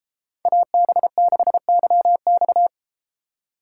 Morse code, heard as a keyed tone, about 32 wpm, 700 Hz. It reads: AB6YX